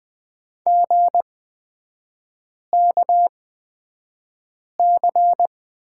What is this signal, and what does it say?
Morse code, heard as a keyed tone, about 20 wpm, spaced as for 5 wpm, 700 Hz. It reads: GKC
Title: 236 Lesson 38e - Letter q - Three Characters 20wpm